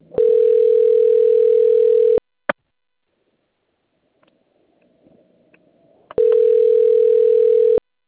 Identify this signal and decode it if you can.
Ringback tone (North America), 440 + 480 Hz.